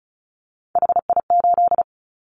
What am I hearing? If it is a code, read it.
Morse code, heard as a keyed tone, about 34 wpm, 700 Hz. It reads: HI8
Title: Callsign Prefixes for Top 100 DXCC - 35wpm